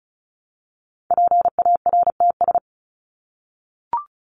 Morse code, heard as a keyed tone, about 34 wpm, 700 Hz. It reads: PARTS
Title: ES:S Suffix - Top 100 Words with Base Words 35wpm